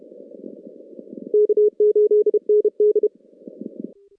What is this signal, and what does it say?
Morse code, heard as a keyed tone, about 30 wpm, 410 Hz. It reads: K8ND
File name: contests / CW-SS-k5tr- -2100